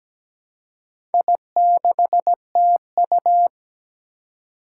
Morse code, heard as a keyed tone, about 17 wpm, 700 Hz. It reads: I6TU
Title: Four Challenging Letter-Number Combinations 17wpm